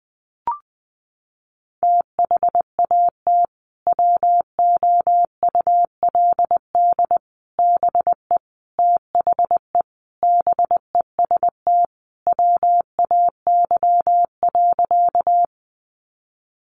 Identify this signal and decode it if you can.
Morse code, heard as a keyed tone, about 20 wpm, 700 Hz. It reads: THAT WOULD BE THE BEST WAY.
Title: Sentences from Top 400 Words 20wpm